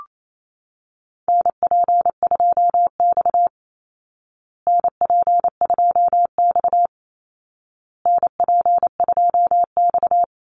Morse code, heard as a keyed tone, about 28 wpm, 700 Hz. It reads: NP2X NP2X NP2X
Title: Call Signs Repeated 2x - 28wpm